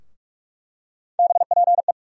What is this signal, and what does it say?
Morse code, heard as a keyed tone, about 43 wpm, 700 Hz. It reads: BPE
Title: USA Call Sign Postfixes - 3 Letters 45wpm